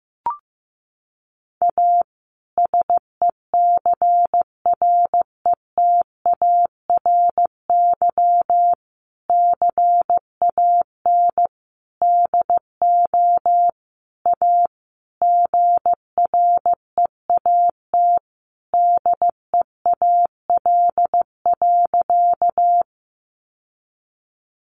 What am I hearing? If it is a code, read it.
Morse code, heard as a keyed tone, about 15 wpm, 700 Hz. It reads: A SECRETARY CAN DO A GREAT DEAL.
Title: Sentences from Top 1300 Words 15wpm